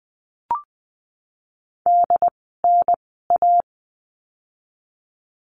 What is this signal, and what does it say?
Morse code, heard as a keyed tone, about 20 wpm, spaced as for 14 wpm, 700 Hz. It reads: DNA
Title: 068 Lesson 12f - Letter d - Words 20wpm